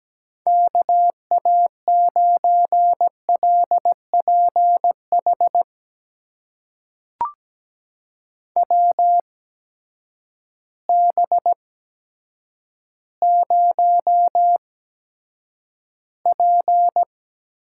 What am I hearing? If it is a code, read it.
Morse code, heard as a keyed tone, about 17 wpm, spaced as for 8 wpm, 700 Hz. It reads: KA9LPH WB0P
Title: USA Call Signs - 6 Characters - 8x Character Spacing - 17wpm